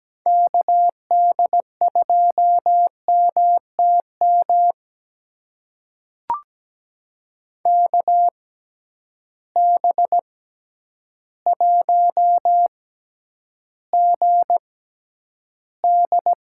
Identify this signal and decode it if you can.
Morse code, heard as a keyed tone, about 17 wpm, spaced as for 9 wpm, 700 Hz. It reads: KD2MTM KB1GD
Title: USA Call Signs - 6 Characters - 6x Character Spacing - 17wpm